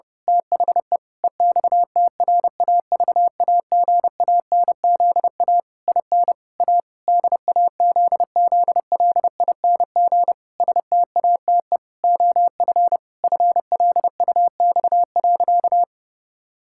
Morse code, heard as a keyed tone, about 30 wpm, 700 Hz. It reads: THE EXTRAVAGANZA IN A DAZZLING STATE OF FLUX.
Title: Pangram Challenge v2 - 30wpm